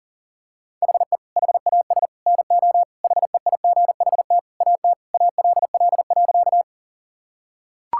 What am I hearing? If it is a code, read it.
Morse code, heard as a keyed tone, about 37 wpm, 700 Hz. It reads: HE HAS NO HEIGHT AT ALL.
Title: Sentences from Top 1600 Words 40wpm